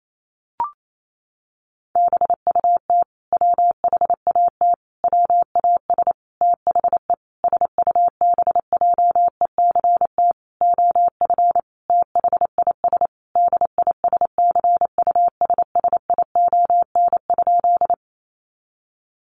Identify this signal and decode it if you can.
Morse code, heard as a keyed tone, about 28 wpm, 700 Hz. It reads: BUT WHAT WAS THE SUBJECT OF THIS DISCUSSION?
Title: Sentences from Top 1100 Words 28wpm